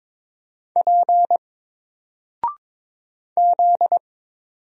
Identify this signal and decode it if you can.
Morse code, heard as a keyed tone, about 22 wpm, 700 Hz. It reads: P Z
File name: Single-Letter-Number-22wpm